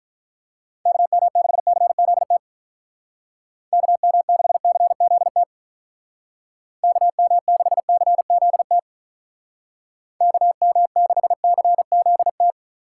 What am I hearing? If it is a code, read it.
Morse code, heard as a keyed tone, about 43 wpm, 700 Hz. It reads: KM6CZT KM6CZT KM6CZT KM6CZT
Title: Speed Racing - USA 2x3 Call Signs 35wpm